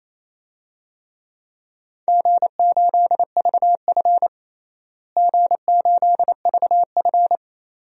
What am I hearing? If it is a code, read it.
Morse code, heard as a keyed tone, about 28 wpm, 700 Hz. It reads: G8VF G8VF
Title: DXCC-100 Call Signs - 4 Characters - Repeated 2x - 28wpm